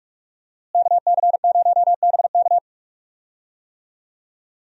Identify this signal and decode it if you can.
Morse code, heard as a keyed tone, about 41 wpm, 700 Hz. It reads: KC0BK
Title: USA Call Signs - 5 Characters - Repeated 3x - 45wpm